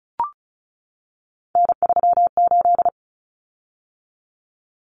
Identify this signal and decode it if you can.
Morse code, heard as a keyed tone, about 34 wpm, 700 Hz. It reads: N38
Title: Common Cut Numbers - 3-Digit 35wpm